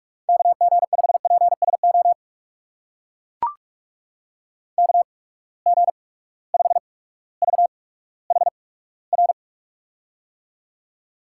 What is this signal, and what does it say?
Morse code, heard as a keyed tone, about 44 wpm, spaced as for 19 wpm, 700 Hz. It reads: KG5PSO KG5VHR
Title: USA Call Signs - 6 Characters - 8x Character Spacing - 45wpm